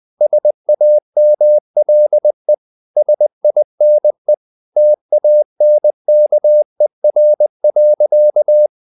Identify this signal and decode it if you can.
Morse code, heard as a keyed tone, about 20 wpm, 600 Hz. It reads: SAMLE SINE TANKER.
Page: Ferie 20 wpm